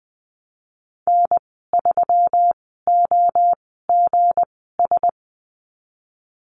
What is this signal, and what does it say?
Morse code, heard as a keyed tone, about 20 wpm, spaced as for 15 wpm, 700 Hz. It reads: N3OGS
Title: 167 Lesson 27g - Number 7 - Call Signs 20wpm